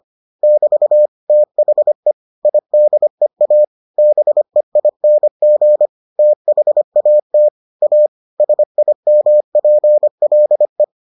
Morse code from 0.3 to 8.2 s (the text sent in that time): <BT> THE IDEA BEING THAT A